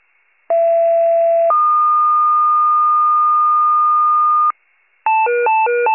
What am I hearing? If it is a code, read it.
Two-tone page, heard as a one-second tone, then a three-second tone, about 670 then 1220 Hz. It is heard from 0.5 to 4.5 s.